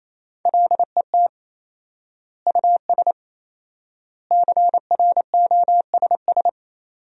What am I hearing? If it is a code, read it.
Morse code, heard as a keyed tone, about 28 wpm, 700 Hz. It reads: LET US CROSS
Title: 4x Word-Spacing - Sentences from Top 300 Words - 28wpm